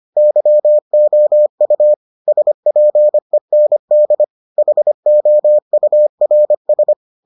Morse code, heard as a keyed tone, about 25 wpm, 600 Hz. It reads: YOU SPEND HOURS